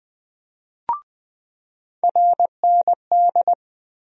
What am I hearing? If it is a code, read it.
Morse code, heard as a keyed tone, about 21 wpm, 700 Hz. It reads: RND
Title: USA Call Sign Postfixes - 3 Letters 20wpm